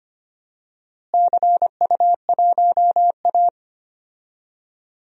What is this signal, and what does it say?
Morse code, heard as a keyed tone, about 25 wpm, 700 Hz. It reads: CU1A
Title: DXCC-100 Call Signs - 4 Characters - Repeated 2x - 25wpm